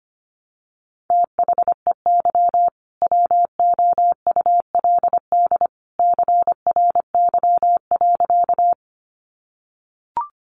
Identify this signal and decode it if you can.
Morse code, heard as a keyed tone, about 25 wpm, 700 Hz. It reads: THEY WOULD CRY.
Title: Sentences from Top 500 Words 25wpm